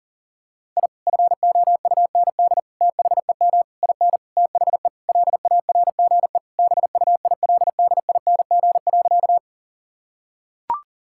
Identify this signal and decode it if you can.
Morse code, heard as a keyed tone, about 37 wpm, 700 Hz. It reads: I FOUND THEM IN THE LARGE BUILDING.
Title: Sentences from Top 1000 Words - Review Part 5 - 40wpm